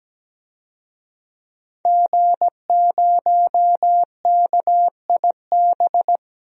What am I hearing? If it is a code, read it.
Morse code, heard as a keyed tone, about 17 wpm, 700 Hz. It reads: G0KIB
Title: DXCC-100 Call Signs - 5 Characters - 17wpm